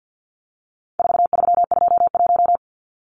Morse code, heard as a keyed tone, about 49 wpm, 700 Hz. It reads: 4321
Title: Numbers - 4-Digit 50wpm